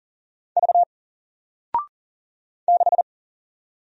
Morse code, heard as a keyed tone, about 39 wpm, 700 Hz. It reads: V 6
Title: Single Letter-Number Alt 40wpm